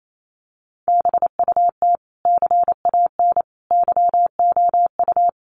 Morse code, heard as a keyed tone, about 28 wpm, 700 Hz. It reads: BUT CAN YOU
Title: Sentences from Top 1400 Words 28wpm